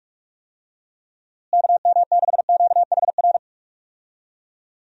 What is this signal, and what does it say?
Morse code, heard as a keyed tone, about 46 wpm, 700 Hz. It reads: KM6QHR
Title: USA Call Signs - 6 Characters - Repeated 2x - 45wpm